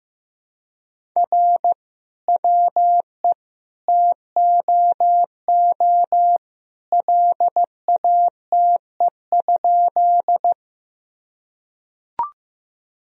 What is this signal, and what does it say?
Morse code, heard as a keyed tone, about 15 wpm, 700 Hz. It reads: R WE TOO LATE?
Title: Sentences from Top 300 Words and Abbreviations 15wpm